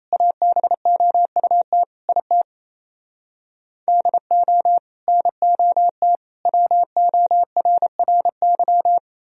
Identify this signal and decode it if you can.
Morse code, heard as a keyed tone, about 37 wpm, 700 Hz. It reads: ABOUT IT DO NOT WORRY